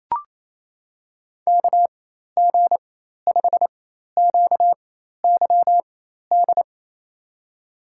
Morse code, heard as a keyed tone, about 28 wpm, spaced as for 13 wpm, 700 Hz. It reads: KG5QYD
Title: USA Call Signs - 4x Character Spacing - 28wpm